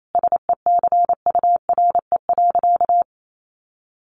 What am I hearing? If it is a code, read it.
Morse code, heard as a keyed tone, about 28 wpm, 700 Hz. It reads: SECURE.
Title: Sentences from Top 1200 Words 28wpm